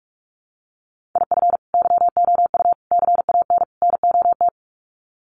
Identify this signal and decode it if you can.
Morse code, heard as a keyed tone, about 44 wpm, 700 Hz. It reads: IF YOU CAN NOT
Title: Sets of 4 Words 45wpm